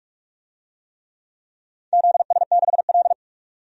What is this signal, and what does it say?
Morse code, heard as a keyed tone, about 46 wpm, 700 Hz. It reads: ZS6L
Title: DXCC-100 Call Signs - 4 Characters - 45wpm